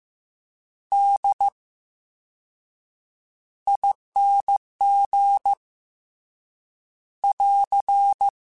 Morse code, heard as a keyed tone, about 15 wpm, spaced as for 7 wpm, 800 Hz. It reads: DING <AR>
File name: Special INGs / Special INGs-mp3s / Slow ing @13